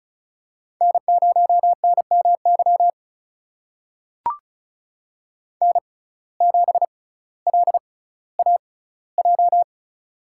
Morse code, heard as a keyed tone, about 34 wpm, spaced as for 18 wpm, 700 Hz. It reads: N0NMY N7LAJ
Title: USA Call Signs - 5 Characters - 6x Character Spacing - 35wpm